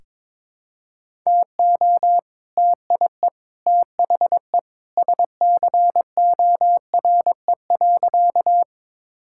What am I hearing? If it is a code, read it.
Morse code, heard as a keyed tone, about 23 wpm, 690 Hz. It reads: TO TIE THE SCORE.
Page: morse-code-ninja-practice-22wpm